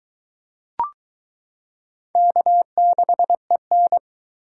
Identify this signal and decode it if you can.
Morse code, heard as a keyed tone, about 22 wpm, 700 Hz. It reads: K6EN